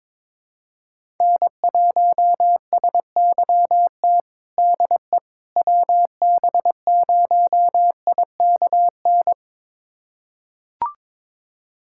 Morse code, heard as a keyed tone, about 22 wpm, 700 Hz. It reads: N1SYT DE WB0IKN
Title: USA Callsign DE Callsign 22wpm